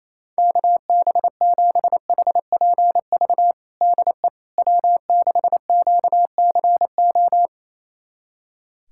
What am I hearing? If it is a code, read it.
Morse code, heard as a keyed tone, about 28 wpm, 700 Hz. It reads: KB7HPV DE W6QCO
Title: USA Callsign DE Callsign 28wpm